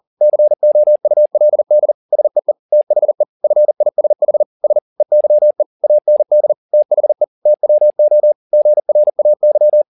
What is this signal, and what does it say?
Morse code, heard as a keyed tone, about 40 wpm, 600 Hz. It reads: COULD SEE THE FISH S EYE AND THE TWO GRAY
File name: the old man and the sea 2 40wpm